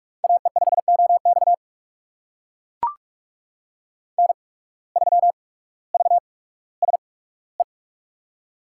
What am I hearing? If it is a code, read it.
Morse code, heard as a keyed tone, about 44 wpm, spaced as for 19 wpm, 700 Hz. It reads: AE5OX N3VSE
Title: USA Call Signs - 8x Character Spacing - 45wpm